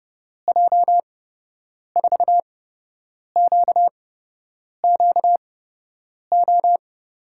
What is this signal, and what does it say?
Morse code, heard as a keyed tone, about 30 wpm, spaced as for 8 wpm, 700 Hz. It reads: J4QQO
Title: USA Call Signs - 6 Characters - 8x Character Spacing - 30wpm